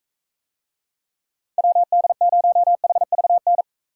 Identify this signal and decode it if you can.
Morse code, heard as a keyed tone, about 44 wpm, 700 Hz. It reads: WD0HVN